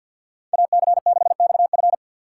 Morse code, heard as a keyed tone, about 52 wpm, 700 Hz. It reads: AC6XF
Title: USA Call Signs - 5 Characters - 4x Character Spacing - 50wpm